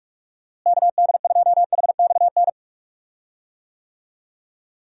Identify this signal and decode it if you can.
Morse code, heard as a keyed tone, about 41 wpm, 700 Hz. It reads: KD2HXN
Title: USA Call Signs - 6 Characters - Repeated 3x - 45wpm